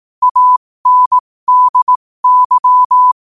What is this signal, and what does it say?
Morse code, heard as a keyed tone, about 18 wpm, spaced as for 15 wpm, 1000 Hz. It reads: ANDY
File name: qotdfiles / 15wpm